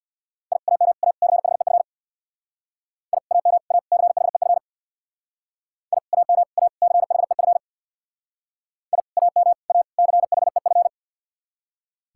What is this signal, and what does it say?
Morse code, heard as a keyed tone, about 66 wpm, 700 Hz. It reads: I AM A CHEF I AM A CHEF I AM A CHEF I AM A CHEF